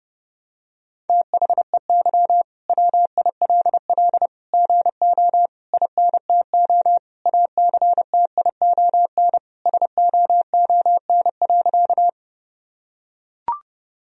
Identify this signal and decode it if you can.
Morse code, heard as a keyed tone, about 30 wpm, 700 Hz. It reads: THEY WILL GO INTO ACTION SOON.
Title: Sentences from Top 1000 Words - Review Part 4 - 30wpm